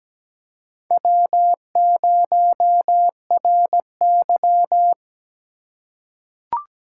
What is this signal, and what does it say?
Morse code, heard as a keyed tone, about 17 wpm, 700 Hz. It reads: W0RY
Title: USA Call Signs - 4 Characters - 2x Character Spacing - 17wpm